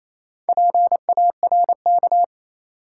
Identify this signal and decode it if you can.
Morse code, heard as a keyed tone, about 28 wpm, 700 Hz. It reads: PARK
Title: Sentences from Top 1300 Words - New Word Drill - 28wpm